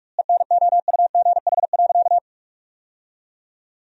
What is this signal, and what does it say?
Morse code, heard as a keyed tone, about 46 wpm, 700 Hz. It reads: ENOUGH.
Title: 2x Word-Spacing - Sentences from Top 400 Words - 45wpm